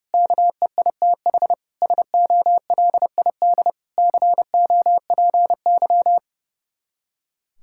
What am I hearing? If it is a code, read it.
Morse code, heard as a keyed tone, about 29 wpm, 700 Hz. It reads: KEITH SOLID COPY
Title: Practice QSO - Exchange 3 - More Info - 30wpm